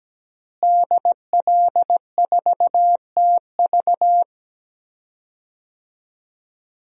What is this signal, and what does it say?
Morse code, heard as a keyed tone, about 17 wpm, 700 Hz. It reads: DL4TV
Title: DXCC-100 Call Signs - 5 Characters - Repeated 3x - 17wpm